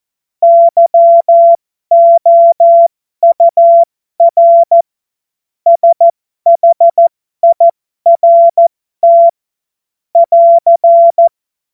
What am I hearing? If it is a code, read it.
Morse code, heard as a keyed tone, about 14 wpm, spaced as for 12 wpm, 680 Hz. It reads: YOUR SHIRT <AR>
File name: Morsekurs / Witze